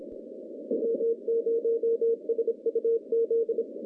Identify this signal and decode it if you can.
Morse code, heard as a keyed tone, about 26 wpm, 460 Hz, with radio noise. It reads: W0SUZ
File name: contests / CW-SS-k5tr- -0200